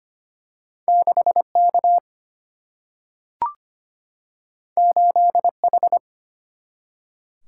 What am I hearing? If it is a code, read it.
Morse code, heard as a keyed tone, about 25 wpm, 700 Hz. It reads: 6 K 8 H
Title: Single-Letter Single-Number 25wpm